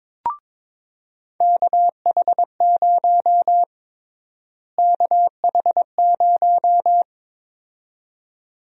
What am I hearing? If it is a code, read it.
Morse code, heard as a keyed tone, about 22 wpm, 700 Hz. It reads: KH0 KH0
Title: USA Call Sign Prefixes - Repeated 2x - 22wpm